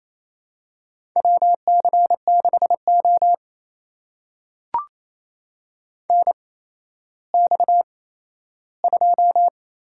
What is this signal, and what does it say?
Morse code, heard as a keyed tone, about 28 wpm, spaced as for 14 wpm, 700 Hz. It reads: WC6O NX2
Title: USA Call Signs - 4 Characters - 8x Character Spacing - 28wpm